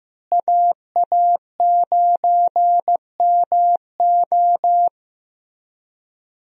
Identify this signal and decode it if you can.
Morse code, heard as a keyed tone, about 15 wpm, 700 Hz. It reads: AA9MO
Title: USA Call Signs - 5 Characters - Repeated 2x - 15wpm